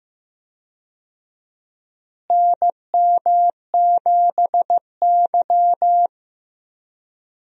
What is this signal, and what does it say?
Morse code, heard as a keyed tone, about 15 wpm, 700 Hz. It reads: NM7Y
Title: USA Call Signs - 4 Characters - Repeated 2x - 15wpm